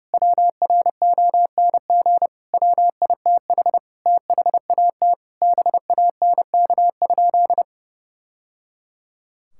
Morse code, heard as a keyed tone, about 30 wpm, 700 Hz. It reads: WRONG WITH THAT BANK?
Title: Sentences from Top 1000 Words - Review Part 6 - 30wpm